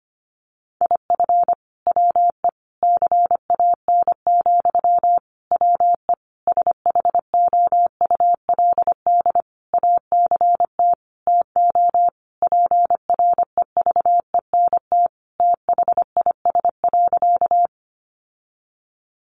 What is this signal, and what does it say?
Morse code, heard as a keyed tone, about 25 wpm, 700 Hz. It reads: IF WE CAN, WE SHOULD ACT TO PREVENT THIS.
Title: Sentences from Top 1100 Words 25wpm